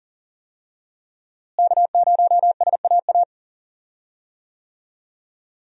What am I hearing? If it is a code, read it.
Morse code, heard as a keyed tone, about 40 wpm, 700 Hz. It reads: K0SAA
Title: USA Call Signs - 5 Characters - Repeated 3x - 40wpm